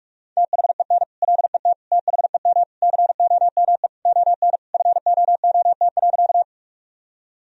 Morse code, heard as a keyed tone, about 46 wpm, 700 Hz. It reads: THEN LET THEM COME ON FOOT.
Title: Sentences from Top 1000 Words - Review Part 3 - 45wpm